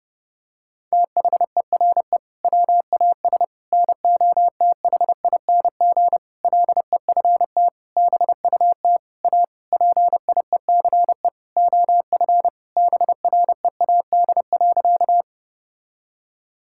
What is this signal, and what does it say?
Morse code, heard as a keyed tone, about 30 wpm, 700 Hz. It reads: THERE WAS NOTHING LEFT BUT A PIECE OF BREAD.
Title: Sentences from Top 1600 Words 30wpm